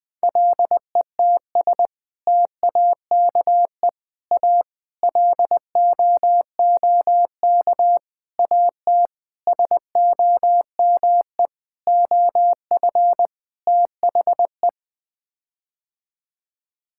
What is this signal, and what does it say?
Morse code, heard as a keyed tone, about 20 wpm, 700 Hz. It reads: LETS TAKE A LOOK AT SOME OF THE
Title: Sets of 8 Words - 20wpm